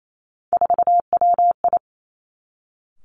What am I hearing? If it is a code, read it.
Morse code, heard as a keyed tone, about 28 wpm, 700 Hz. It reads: 4WI
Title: Three Challenging Letter-Number Combinations 28wpm